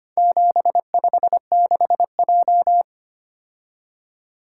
Morse code, heard as a keyed tone, about 25 wpm, 700 Hz. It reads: 756J
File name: practice / 25wpm / Four Letter-Number Combinations - 25wpm